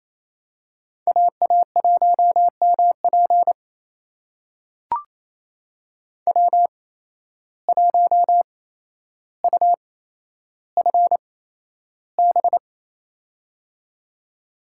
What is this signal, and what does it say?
Morse code, heard as a keyed tone, about 28 wpm, spaced as for 12 wpm, 700 Hz. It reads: AA1MP W1UFB
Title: USA Call Signs - 5 Characters - 8x Character Spacing - 28wpm